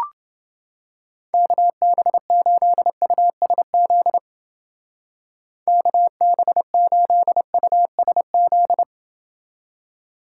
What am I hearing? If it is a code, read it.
Morse code, heard as a keyed tone, about 29 wpm, 700 Hz. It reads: KB8USZ KB8USZ